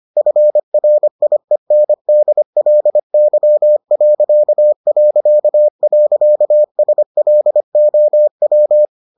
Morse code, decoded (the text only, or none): FRIENDLY...SLOW